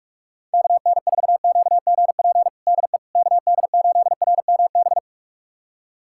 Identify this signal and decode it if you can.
Morse code, heard as a keyed tone, about 46 wpm, 700 Hz. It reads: KN4QGP DE KD8RMB
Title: USA Callsign DE Callsign 45wpm